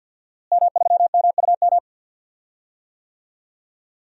Morse code, heard as a keyed tone, about 53 wpm, 700 Hz. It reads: M3MUM
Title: DXCC-100 Call Signs - 5 Characters - Repeated 2x - 50wpm